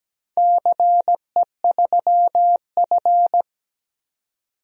Morse code, heard as a keyed tone, about 17 wpm, 700 Hz. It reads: CE3F
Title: DXCC-100 Call Signs - 4 Characters - 17wpm